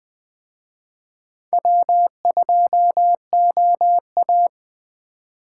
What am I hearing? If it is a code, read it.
Morse code, heard as a keyed tone, about 23 wpm, 700 Hz. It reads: W2OA